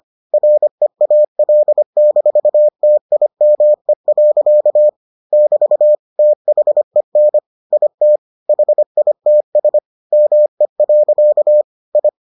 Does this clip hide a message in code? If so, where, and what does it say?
Yes: Morse code from 0.3 to 12.1 s, heard as a keyed tone, about 25 wpm, 600 Hz. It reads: REAL-TIME. <BT> THEN IT HITS ME. I